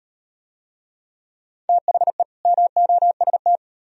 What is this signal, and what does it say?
Morse code, heard as a keyed tone, about 37 wpm, 700 Hz. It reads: THE MOST